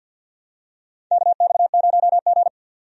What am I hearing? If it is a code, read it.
Morse code, heard as a keyed tone, about 49 wpm, 700 Hz. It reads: KX0G
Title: USA Call Signs - 4 Characters - 50wpm